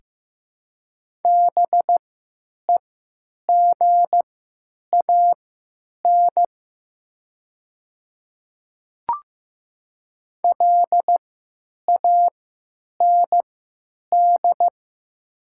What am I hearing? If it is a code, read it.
Morse code, heard as a keyed tone, about 15 wpm, spaced as for 9 wpm, 700 Hz. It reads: BEGAN LAND
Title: 3x Character Spacing - Top 500 Words - 15wpm